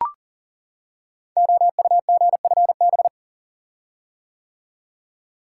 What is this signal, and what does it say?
Morse code, heard as a keyed tone, about 38 wpm, 690 Hz. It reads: OUGFB
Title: Code Groups - 5 Letters - 40wpm